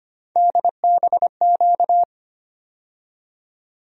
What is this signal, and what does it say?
Morse code, heard as a keyed tone, about 25 wpm, 700 Hz. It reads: DBQ
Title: USA Call Sign Postfixes - 3 Letters 25wpm